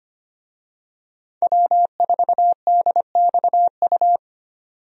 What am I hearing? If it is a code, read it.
Morse code, heard as a keyed tone, about 25 wpm, 700 Hz. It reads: W4DXU
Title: USA Call Signs - 5 Characters - 25wpm